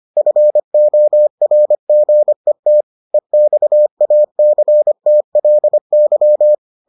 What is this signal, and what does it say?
Morse code, heard as a keyed tone, about 25 wpm, 600 Hz. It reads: FORGET EXACTLY